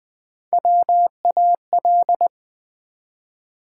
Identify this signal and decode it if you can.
Morse code, heard as a keyed tone, about 20 wpm, 700 Hz. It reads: WAL
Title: Build A Word - 1 Letter At A Time - 4-Letter Words - 20wpm